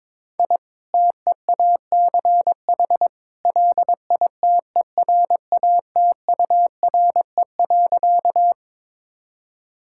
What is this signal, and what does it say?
Morse code, heard as a keyed tone, about 22 wpm, 700 Hz. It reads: I TEACH LITERATURE.
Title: Sentences from Top 1000 Words - Review Part 6 - 22wpm